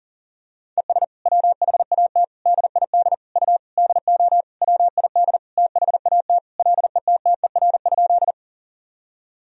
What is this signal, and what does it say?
Morse code, heard as a keyed tone, about 40 wpm, 700 Hz. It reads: ES WHAT DID U DO WID THAT LETTER?
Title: Sentences from Top 300 Words and Abbreviations 40wpm